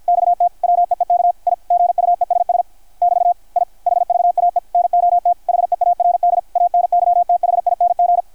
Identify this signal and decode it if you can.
Morse code, heard as a keyed tone, about 53 wpm, 690 Hz. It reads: XT WEEK I GUESS <BT> I HVAE NOT HEARD ANYTHING
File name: archived / CRSnet